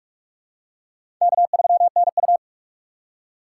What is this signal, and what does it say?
Morse code, heard as a keyed tone, about 46 wpm, 700 Hz. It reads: K3NU
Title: USA Call Signs - 4 Characters - 3x Character Spacing - 45wpm